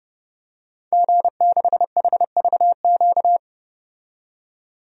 Morse code, heard as a keyed tone, about 32 wpm, 700 Hz. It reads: G6HVQ
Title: DXCC-100 Call Signs - 5 Characters - 30wpm